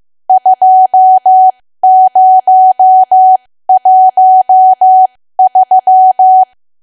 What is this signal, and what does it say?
Morse code, heard as a keyed tone, about 15 wpm, spaced as for 13 wpm, 750 Hz. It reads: 2013
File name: cwa / W1AW-Level2 / 13WPM